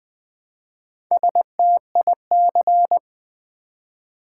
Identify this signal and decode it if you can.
Morse code, heard as a keyed tone, about 20 wpm, 700 Hz. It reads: STIC
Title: Build A Word - 1 Letter At A Time - 5-Letter Words - 20wpm